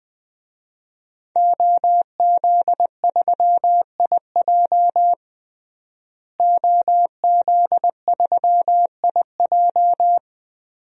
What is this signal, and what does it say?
Morse code, heard as a keyed tone, about 21 wpm, 700 Hz. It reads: OZ3IJ OZ3IJ
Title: DXCC-100 Call Signs - 5 Characters - Repeated 3x - 20wpm